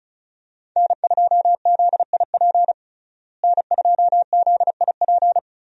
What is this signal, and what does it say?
Morse code, heard as a keyed tone, about 36 wpm, 700 Hz. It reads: N2ZIP N2ZIP
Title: USA Call Signs - 5 Characters - Repeated 3x - 35wpm